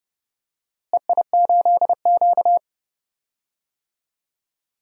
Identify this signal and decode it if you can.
Morse code, heard as a keyed tone, about 30 wpm, 700 Hz. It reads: EI8Q
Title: DXCC-100 Call Signs - 4 Characters - Repeated 2x - 30wpm